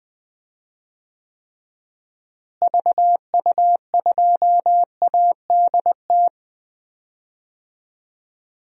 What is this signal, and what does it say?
Morse code, heard as a keyed tone, about 20 wpm, 700 Hz. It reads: VU2ADT
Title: DXCC-100 Call Signs - 6 Characters - Repeated 2x - 20wpm